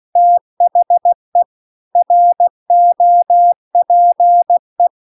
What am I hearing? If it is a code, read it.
Morse code, heard as a keyed tone, about 16 wpm, 700 Hz. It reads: THE ROPE